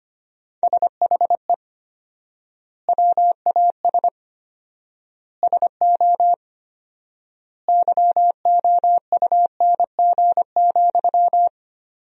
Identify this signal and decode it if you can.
Morse code, heard as a keyed tone, about 25 wpm, 700 Hz. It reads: SHE WAS SO YOUNG,
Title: 4x Word-Spacing - Sentences from Top 400 Words - 25wpm